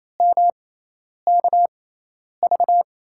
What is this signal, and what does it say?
Morse code, heard as a keyed tone, about 28 wpm, spaced as for 10 wpm, 700 Hz. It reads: MKV
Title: USA Call Signs - 6 Characters - 6x Character Spacing - 28wpm